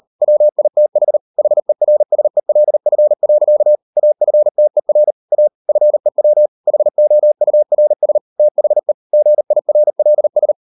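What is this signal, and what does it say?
Morse code, heard as a keyed tone, about 38 wpm, 600 Hz. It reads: WITH HERSELF. AFTER A FEW HOURS THE GIRLS
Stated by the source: Morse code practice at 38 WPM, sending Ken Follett's novel Never in English, from CW med Gnister.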